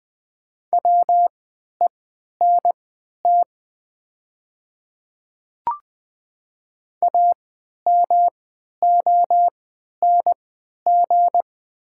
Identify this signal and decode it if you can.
Morse code, heard as a keyed tone, about 20 wpm, spaced as for 11 wpm, 700 Hz. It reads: WENT AMONG
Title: 3x Character Spacing - Top 500 Words - 20wpm